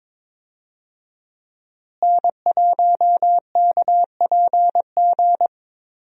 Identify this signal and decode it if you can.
Morse code, heard as a keyed tone, about 22 wpm, 700 Hz. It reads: N1KPG
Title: USA Call Signs - 5 Characters - Repeated 2x - 22wpm